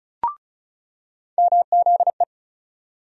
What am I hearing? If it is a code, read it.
Morse code, heard as a keyed tone, about 34 wpm, 700 Hz. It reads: MZE
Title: USA Call Sign Postfixes - 3 Letters 35wpm